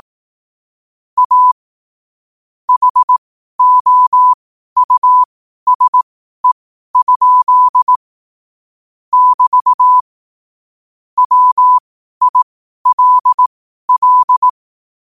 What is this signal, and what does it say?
Morse code, heard as a keyed tone, about 18 wpm, spaced as for 12 wpm, 1000 Hz. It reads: A HOUSE? <BT> WILL
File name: qotdfiles / 12wpm